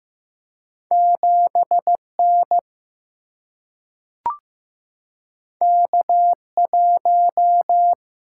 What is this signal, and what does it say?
Morse code, heard as a keyed tone, about 15 wpm, 700 Hz. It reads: 7 N K 1
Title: Single-Letter Single-Number 15wpm